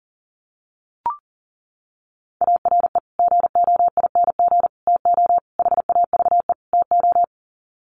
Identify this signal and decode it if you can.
Morse code, heard as a keyed tone, about 40 wpm, 700 Hz. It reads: ARE GOING TO HAVE TO
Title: Sets of 5 Words 40wpm